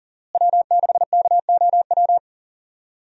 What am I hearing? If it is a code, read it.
Morse code, heard as a keyed tone, about 40 wpm, 700 Hz. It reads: W6KOW
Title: USA Call Signs - 5 Characters - Repeated 2x - 40wpm